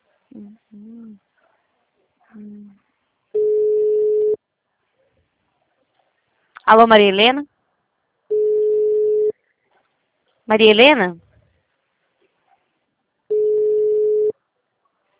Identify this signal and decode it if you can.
Ringback tone (Europe, 425 Hz), with speech.
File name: out- -2025- - - .72444